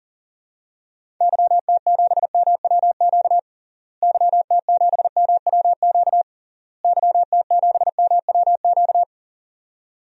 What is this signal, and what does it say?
Morse code, heard as a keyed tone, about 40 wpm, 700 Hz. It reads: YT7MWQ YT7MWQ YT7MWQ
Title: DXCC-100 Call Signs - 6 Characters - Repeated 3x - 40wpm